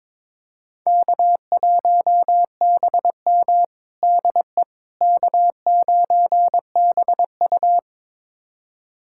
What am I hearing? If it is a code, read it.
Morse code, heard as a keyed tone, about 21 wpm, 700 Hz. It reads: K1BM DE K9BU
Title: USA Callsign DE Callsign - 4 Characters - 22wpm